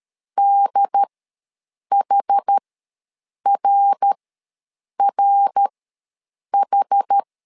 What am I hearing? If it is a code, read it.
Morse code, heard as a keyed tone, about 13 wpm, spaced as for 7 wpm, 800 Hz. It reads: DHRRH
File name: LEZIONI / 3 MWRO / 3 MWR ws 12wpm 1809